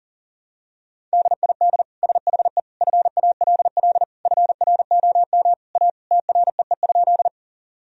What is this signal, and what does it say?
Morse code, heard as a keyed tone, about 37 wpm, 700 Hz. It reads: DID SHE FALL FROM A TREE?
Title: Sentences from Top 1000 Words - Review Part 2 - 40wpm